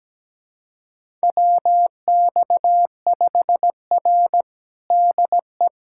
Morse code, heard as a keyed tone, about 17 wpm, 700 Hz. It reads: WX5R DE
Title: USA Callsign DE Callsign - 4 Characters - 17wpm